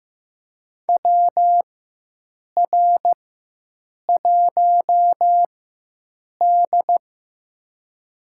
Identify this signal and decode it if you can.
Morse code, heard as a keyed tone, about 15 wpm, spaced as for 7 wpm, 700 Hz. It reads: WR1D
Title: USA Call Signs - 4 Characters - 4x Character Spacing - 15wpm